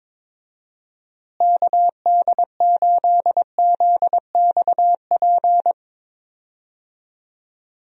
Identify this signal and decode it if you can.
Morse code, heard as a keyed tone, about 22 wpm, 700 Hz. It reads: KD8ZXP
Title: USA Call Signs - 6 Characters - Repeated 3x - 22wpm